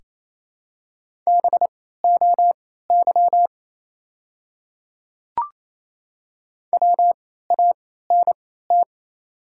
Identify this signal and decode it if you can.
Morse code, heard as a keyed tone, about 28 wpm, spaced as for 16 wpm, 700 Hz. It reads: BOY WANT
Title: 3x Character Spacing - Top 200 Words - 28wpm